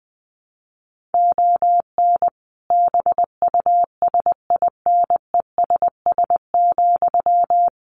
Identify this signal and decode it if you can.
Morse code, heard as a keyed tone, about 20 wpm, 700 Hz. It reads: ON BUSINESS,
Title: Sentences from Top 1300 Words 20wpm